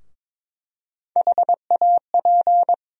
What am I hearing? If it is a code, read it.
Morse code, heard as a keyed tone, about 22 wpm, 700 Hz. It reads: HAP
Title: USA Call Sign Postfixes - 3 Letters 22wpm